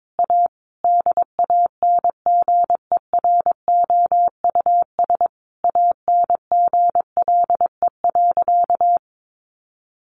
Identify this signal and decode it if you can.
Morse code, heard as a keyed tone, about 22 wpm, 700 Hz. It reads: A DANGEROUS ANGLE.
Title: Sentences from Top 1500 Words 22wpm